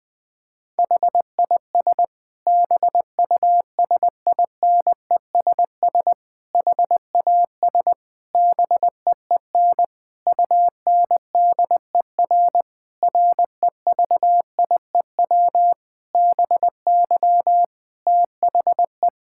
Morse code, heard as a keyed tone, about 20 wpm, 700 Hz. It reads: HIS BUSINESS HAS BEEN UNDER REVIEW BY THE
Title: Sentences from Top 1000 Words - Review Part 6 - 20wpm